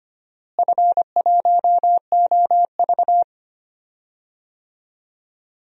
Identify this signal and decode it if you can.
Morse code, heard as a keyed tone, about 25 wpm, 700 Hz. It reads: F1OV
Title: DXCC-100 Call Signs - 4 Characters - Repeated 3x - 25wpm